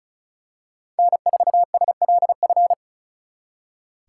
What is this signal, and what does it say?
Morse code, heard as a keyed tone, about 33 wpm, 700 Hz. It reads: N4SLF
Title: Speed Racing - USA 1x3 Call Signs 35wpm